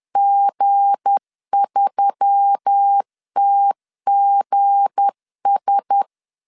Morse code, heard as a keyed tone, about 10 wpm, 800 Hz. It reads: G3TGS
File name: LEZIONI / 7 GF9Z / 7 G 10wpm 2501